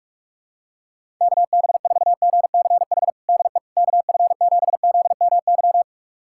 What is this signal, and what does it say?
Morse code, heard as a keyed tone, about 46 wpm, 700 Hz. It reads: KB4GCH DE KF7ZMY
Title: USA Callsign DE Callsign 45wpm